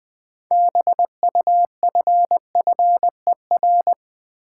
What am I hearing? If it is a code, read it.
Morse code, heard as a keyed tone, about 20 wpm, 700 Hz. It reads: BUFFER
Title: Build A Word - 3 Letters At A Time - 6-Letter Words - 20wpm